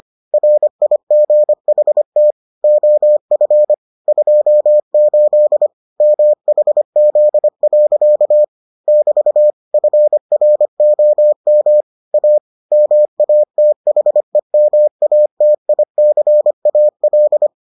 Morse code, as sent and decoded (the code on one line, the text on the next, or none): .-. .. --. .... - / --- ..-. / ..--- ---.. / -- .... --.. .-.-.- / -...- / ..-. .-. --- -- / .- / -- .- - .... -- .- - .. -.-. .- .-..
RIGHT OF 28 MHZ. <BT> FROM A MATHEMATICAL